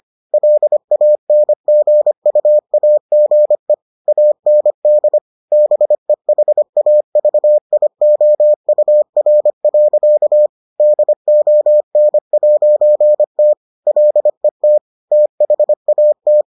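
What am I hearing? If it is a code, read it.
Morse code, heard as a keyed tone, about 25 wpm, 600 Hz. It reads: LANGUAGE AND BEHAVIOUR. DON'T LET THAT
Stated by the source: Morse edition of the Foundations of Amateur Radio episodes.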